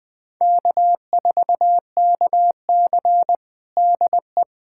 Morse code, heard as a keyed tone, about 21 wpm, 700 Hz. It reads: K4KC DE
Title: USA Callsign DE Callsign - 4 Characters - 20wpm